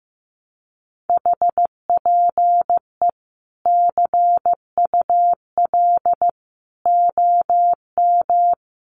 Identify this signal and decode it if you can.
Morse code, heard as a keyed tone, about 15 wpm, 700 Hz. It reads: HPE CUL OM
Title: Common QSO Phrases 15wpm